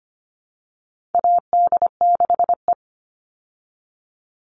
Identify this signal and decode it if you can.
Morse code, heard as a keyed tone, about 25 wpm, 700 Hz. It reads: AD6E
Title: Call Signs Repeated 3x - 25wpm